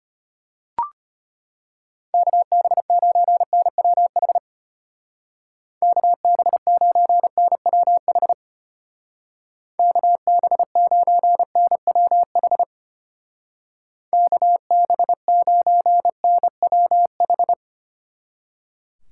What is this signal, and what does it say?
Morse code, heard as a keyed tone, about 30 wpm, 700 Hz. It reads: KB9NWH KB9NWH KB9NWH KB9NWH
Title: Speed Racing - USA 2x3 Call Signs 25wpm